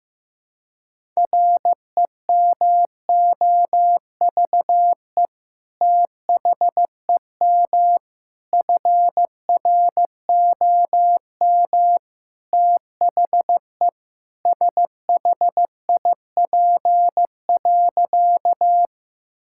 Morse code, heard as a keyed tone, about 15 wpm, 700 Hz. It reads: REMOVE THEM FROM THE SHIP.
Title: Sentences from Top 1000 Words - Review Part 5 - 15wpm